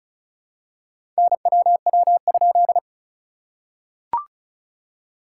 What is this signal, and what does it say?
Morse code, heard as a keyed tone, about 34 wpm, 660 Hz. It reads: NWW?
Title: SOTA QSO Practice 35wpm